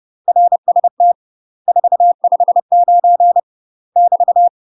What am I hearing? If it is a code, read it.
Morse code, heard as a keyed tone, about 30 wpm, 700 Hz. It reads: RST 459 <BT>
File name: UK MASTER QSOs TRAINING / 30WPM / UK MASTER QSOs 0042